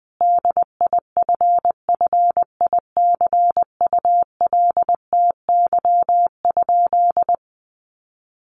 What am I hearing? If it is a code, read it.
Morse code, heard as a keyed tone, about 20 wpm, 700 Hz. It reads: DIFFICULTY?
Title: Sentences from Top 1500 Words 20wpm